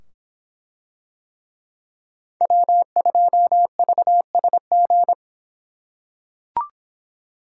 Morse code, heard as a keyed tone, about 25 wpm, 700 Hz. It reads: W2VSG